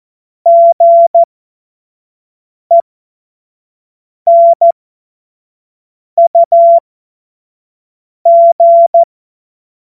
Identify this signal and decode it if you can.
Morse code, heard as a keyed tone, about 14 wpm, spaced as for 5 wpm, 680 Hz. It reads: GENUG
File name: Morsekurs / Witze / Witz 01. August 2023 25 F70